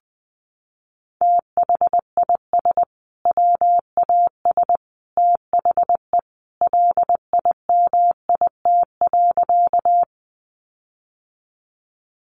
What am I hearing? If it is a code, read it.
Morse code, heard as a keyed tone, about 20 wpm, 700 Hz. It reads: THIS WAS THE LIMIT.
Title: Sentences from Top 1500 Words 20wpm